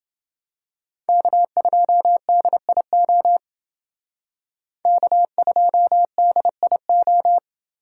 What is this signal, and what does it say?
Morse code, heard as a keyed tone, about 29 wpm, 700 Hz. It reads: K2DIO K2DIO